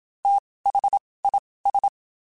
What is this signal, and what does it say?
Morse code, heard as a keyed tone, about 27 wpm, spaced as for 19 wpm, 800 Hz. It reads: THIS
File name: (spaced) / QSOs Spaced-mp3s / WD0FBT @25 sp